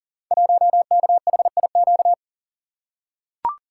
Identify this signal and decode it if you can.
Morse code, heard as a keyed tone, about 39 wpm, 700 Hz. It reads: 1KHIQ
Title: Code Groups - 5 Letter:Numbers - 40wpm